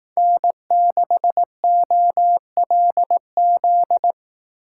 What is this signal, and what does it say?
Morse code, heard as a keyed tone, about 18 wpm, 700 Hz. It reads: N6OLZ
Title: Speed Racing - USA 1x3 Call Signs 15wpm